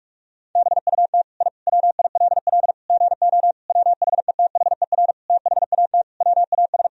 Morse code, heard as a keyed tone, about 44 wpm, 700 Hz. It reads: BUT I WILL GO WHETHER THAT WAS
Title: Sentences from Top 1000 Words - Review Part 3 - 45wpm